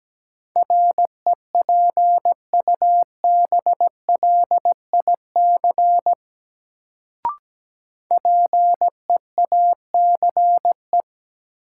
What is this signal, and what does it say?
Morse code, heard as a keyed tone, about 17 wpm, 700 Hz. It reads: REPUBLIC PEACE
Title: Sentences from Top 1500 Words - New Word Drill - 17wpm